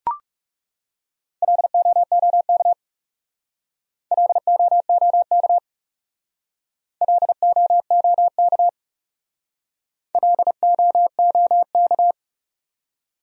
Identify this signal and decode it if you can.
Morse code, heard as a keyed tone, about 40 wpm, 700 Hz. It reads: LOOK LOOK LOOK LOOK